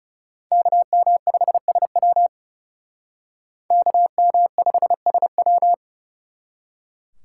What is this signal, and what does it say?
Morse code, heard as a keyed tone, about 38 wpm, 700 Hz. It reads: KM5SW KM5SW